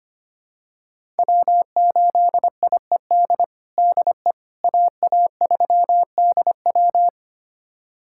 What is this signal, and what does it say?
Morse code, heard as a keyed tone, about 25 wpm, 700 Hz. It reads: W8IED DE AA3DW
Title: USA Callsign DE Callsign - 5 Characters - 25wpm